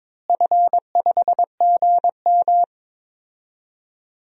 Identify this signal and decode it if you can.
Morse code, heard as a keyed tone, about 22 wpm, 700 Hz. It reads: F5GM
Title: DXCC-100 Call Signs - 4 Characters - Repeated 3x - 22wpm